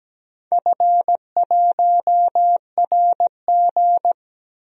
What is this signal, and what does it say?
Morse code, heard as a keyed tone, about 17 wpm, 700 Hz. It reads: F1RG
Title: DXCC-100 Call Signs - 4 Characters - 17wpm